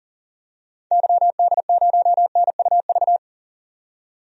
Morse code, heard as a keyed tone, about 40 wpm, 700 Hz. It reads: YD0NUV
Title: DXCC-100 Call Signs - 6 Characters - 40wpm